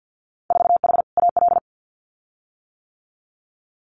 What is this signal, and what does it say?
Morse code, heard as a keyed tone, about 50 wpm, 700 Hz. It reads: 4H AL
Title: Winter Field Day Exchanges Encore 50wpm